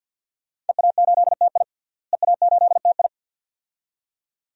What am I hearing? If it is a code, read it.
Morse code, heard as a keyed tone, about 50 wpm, 700 Hz. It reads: EA8TI EA8TI
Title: DXCC-100 Call Signs - 5 Characters - Repeated 2x - 50wpm